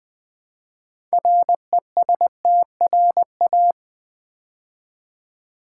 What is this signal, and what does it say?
Morse code, heard as a keyed tone, about 20 wpm, 700 Hz. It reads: RESTRA
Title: Build A Word - 3 Letters At A Time - 9-Letter Words - 20wpm